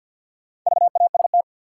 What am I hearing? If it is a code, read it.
Morse code, heard as a keyed tone, about 53 wpm, 700 Hz. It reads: VAST
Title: Sentences from Top 1300 Words - New Word Drill Alt - 50wpm